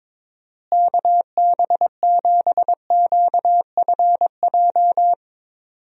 Morse code, heard as a keyed tone, about 22 wpm, 700 Hz. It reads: KB7QFJ
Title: USA Call Signs - 6 Characters - Repeated 2x - 22wpm